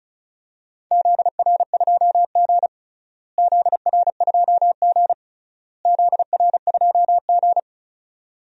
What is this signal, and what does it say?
Morse code, heard as a keyed tone, about 36 wpm, 700 Hz. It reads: ZR2G ZR2G ZR2G
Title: DXCC-100 Call Signs - 4 Characters - Repeated 3x - 35wpm